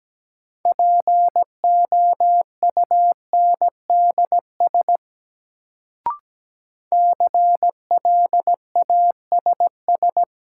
Morse code, heard as a keyed tone, about 17 wpm, 700 Hz. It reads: POUNDS CLASS
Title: Sentences from Top 1000 Words - Review Part 2 - New Word Drill - 17wpm